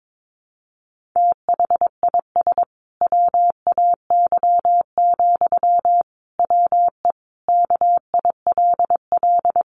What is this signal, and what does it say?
Morse code, heard as a keyed tone, about 22 wpm, 700 Hz. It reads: THIS WAY, WE KILL
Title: A Sound Of Thunder 22wpm